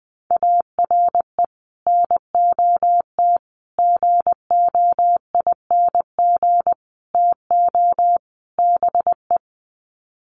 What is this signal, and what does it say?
Morse code, heard as a keyed tone, about 20 wpm, 700 Hz. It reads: ARE NOT GOING TO BE
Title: Sets of 5 Words Encore 20wpm